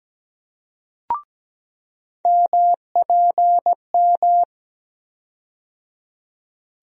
Morse code, heard as a keyed tone, about 17 wpm, 700 Hz. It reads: MPM
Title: USA Call Sign Postfixes - 3 Letters 17wpm